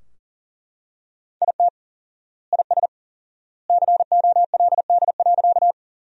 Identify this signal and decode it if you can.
Morse code, heard as a keyed tone, about 37 wpm, 700 Hz. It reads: IT IS COLD.
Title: 4x Word-Spacing - Sentences from Top 400 Words - 40wpm